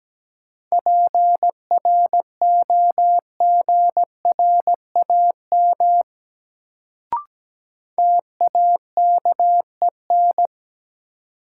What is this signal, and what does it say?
Morse code, heard as a keyed tone, about 17 wpm, 700 Hz. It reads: PROGRAM TAKEN
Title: Sentences from Top 1000 Words - Review Part 4 - New Word Drill - 17wpm